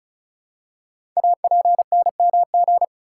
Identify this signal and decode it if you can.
Morse code, heard as a keyed tone, about 36 wpm, 700 Hz. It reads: APNMG
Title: Code Groups - 5 Letters - 35wpm